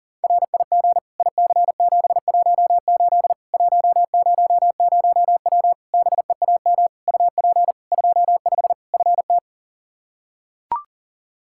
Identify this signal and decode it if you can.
Morse code, heard as a keyed tone, about 40 wpm, 700 Hz. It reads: RIG IC718 100W BEAM UP 25 FT
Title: Practice QSO - Exchange 2 - Info - 40wpm